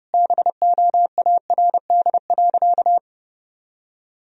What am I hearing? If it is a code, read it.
Morse code, heard as a keyed tone, about 30 wpm, 700 Hz. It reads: BOARD.
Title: Sentences from Top 1000 Words - Review Part 4 - 30wpm